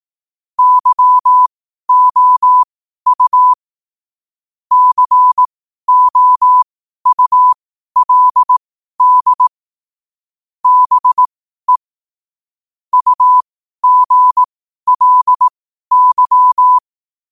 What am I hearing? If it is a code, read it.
Morse code, heard as a keyed tone, about 18 wpm, spaced as for 12 wpm, 1000 Hz. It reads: YOU COULD BE UGLY
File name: qotdfiles / 12wpm